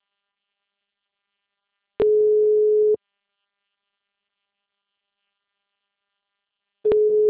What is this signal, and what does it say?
Ringback tone (Europe, 425 Hz).
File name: out- -2003- - - .152